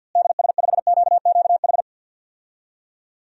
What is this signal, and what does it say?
Morse code, heard as a keyed tone, about 51 wpm, 700 Hz. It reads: DS5QQH
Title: DXCC-100 Call Signs - 6 Characters - Repeated 3x - 50wpm